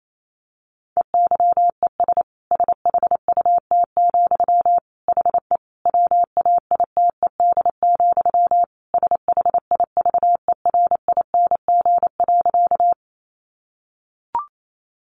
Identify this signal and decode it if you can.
Morse code, heard as a keyed tone, about 28 wpm, 700 Hz. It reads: EYES SHUT, HE WAITED, SHIVERING.
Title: A Sound Of Thunder 28wpm